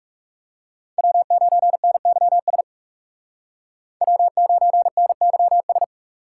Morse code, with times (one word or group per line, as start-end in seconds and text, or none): 1.0-2.6 W9NYS
4.0-5.8 W9NYS